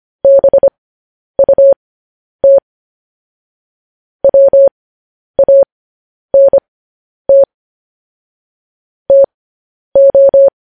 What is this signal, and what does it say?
Morse code, heard as a keyed tone, about 25 wpm, spaced as for 10 wpm, 550 Hz. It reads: BUT WANT TO
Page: news headlines in morse code at 10 wpm